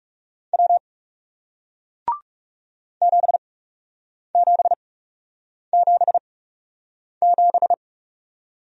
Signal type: Morse code

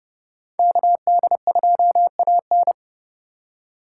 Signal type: Morse code

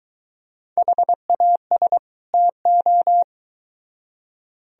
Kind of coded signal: Morse code

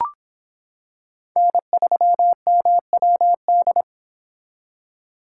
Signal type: Morse code

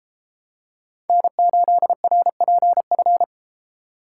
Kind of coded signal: Morse code

